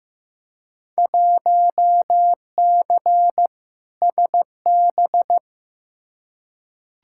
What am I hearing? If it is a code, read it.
Morse code, heard as a keyed tone, about 15 wpm, 700 Hz. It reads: 1C SB
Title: Field Day Exchanges Encore - 15wpm